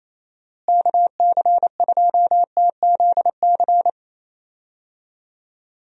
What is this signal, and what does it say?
Morse code, heard as a keyed tone, about 28 wpm, 700 Hz. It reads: KC2TZC
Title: USA Call Signs - 6 Characters - Repeated 2x - 28wpm